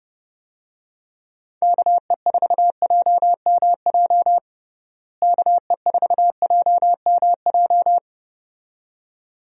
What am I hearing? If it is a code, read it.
Morse code, heard as a keyed tone, about 32 wpm, 700 Hz. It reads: KE4JMJ KE4JMJ
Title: USA Call Signs - 6 Characters - Repeated 2x - 30wpm